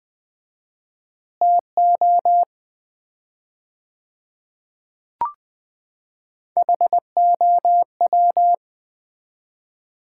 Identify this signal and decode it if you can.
Morse code, heard as a keyed tone, about 21 wpm, 700 Hz. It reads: TO HOW
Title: 4x Word-Spacing - Sets of 5 Words - 20wpm